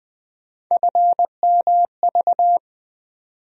Morse code, heard as a keyed tone, about 20 wpm, 700 Hz. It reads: FMV
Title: USA Call Sign Postfixes - 3 Letters 20wpm